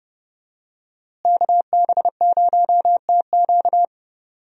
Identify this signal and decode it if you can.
Morse code, heard as a keyed tone, about 29 wpm, 700 Hz. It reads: KB0TQ
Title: USA Call Signs - 5 Characters - Repeated 3x - 30wpm